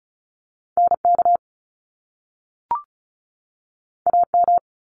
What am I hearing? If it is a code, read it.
Morse code, heard as a keyed tone, about 36 wpm, 700 Hz. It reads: NK AM
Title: Top Two Letter Combinations in Words 35wpm